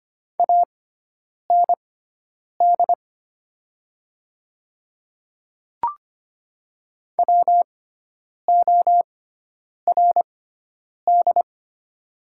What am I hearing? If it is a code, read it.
Morse code, heard as a keyed tone, about 25 wpm, spaced as for 9 wpm, 700 Hz. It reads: AND WORD
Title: 6x Character Spacing - Top 100 Words - 25wpm